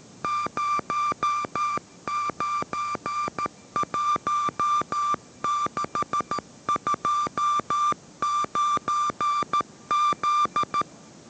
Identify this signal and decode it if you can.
Morse code, heard as a keyed tone, about 13 wpm, 1180 Hz. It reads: 091629Z
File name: 14wpm